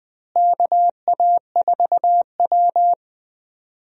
Morse code, heard as a keyed tone, about 20 wpm, 700 Hz. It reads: KA4W
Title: USA Call Signs - 4 Characters - Repeated 3x - 20wpm